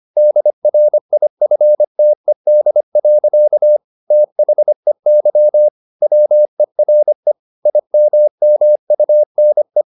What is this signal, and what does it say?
Morse code, heard as a keyed tone, about 25 wpm, 600 Hz. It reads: DRIFTED. THEY WERE IMMUNE
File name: the old man and the sea part 1 25wpm